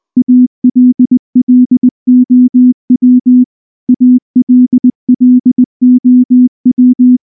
Morse code, heard as a keyed tone, about 18 wpm, 250 Hz. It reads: ALLOW ALLOW
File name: morse / morse-training-20WPM-5